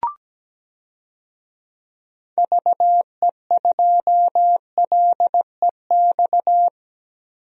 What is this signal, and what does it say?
Morse code, heard as a keyed tone, about 17 wpm, 700 Hz. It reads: VE2LEX
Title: DXCC-100 Call Signs - 6 Characters - Repeated 2x - 17wpm